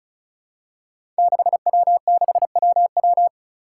Morse code, heard as a keyed tone, about 36 wpm, 700 Hz. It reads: 6W6WW
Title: Five Challenging Letter-Number Combinations 35wpm